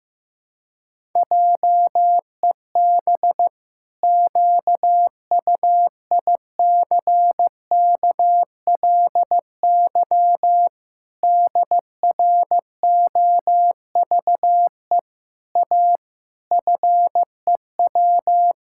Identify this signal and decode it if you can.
Morse code, heard as a keyed tone, about 15 wpm, 700 Hz. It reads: JEB QUICKLY DROVE A FEW
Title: Pangram Challenge v2 - 15wpm